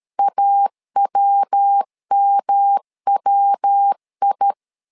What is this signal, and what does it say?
Morse code, heard as a keyed tone, about 13 wpm, 800 Hz. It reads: AWMWI
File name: LEZIONI / 3 MWRO / 3 MW 12wpm 1633